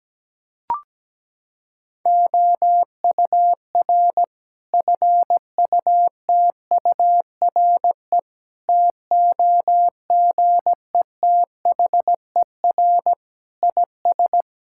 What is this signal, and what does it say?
Morse code, heard as a keyed tone, about 17 wpm, 700 Hz. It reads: OUR FUTURE TOGETHER IS
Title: Sentences from Top 1000 Words - Review Part 5 - 17wpm